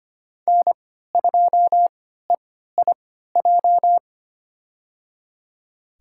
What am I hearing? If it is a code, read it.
Morse code, heard as a keyed tone, about 25 wpm, spaced as for 14 wpm, 700 Hz. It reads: N2EIJ
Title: USA Call Signs - 5 Characters - 3x Character Spacing - 25wpm